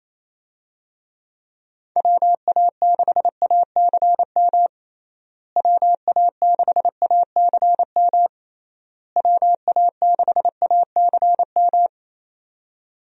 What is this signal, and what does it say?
Morse code, heard as a keyed tone, about 28 wpm, 700 Hz. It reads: WA6ACM WA6ACM WA6ACM
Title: USA Call Signs - 6 Characters - Repeated 3x - 28wpm